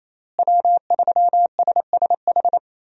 Morse code, heard as a keyed tone, about 28 wpm, 700 Hz. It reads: W3SSH
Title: USA Call Signs - 5 Characters - Repeated 2x - 28wpm